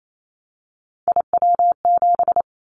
Morse code, heard as a keyed tone, about 28 wpm, 700 Hz. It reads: IW7
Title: Callsign Prefixes for Top 25 DXCC - 28wpm